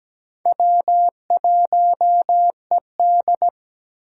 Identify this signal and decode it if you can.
Morse code, heard as a keyed tone, about 17 wpm, 700 Hz. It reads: W1ED
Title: USA Call Signs - 4 Characters - 8x Character Spacing - 17wpm